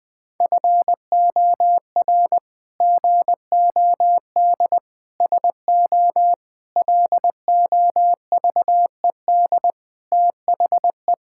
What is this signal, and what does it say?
Morse code, heard as a keyed tone, about 20 wpm, 700 Hz. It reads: FOR GOD SO LOVED THE
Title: Sentences with 8-Word Phrases - 20wpm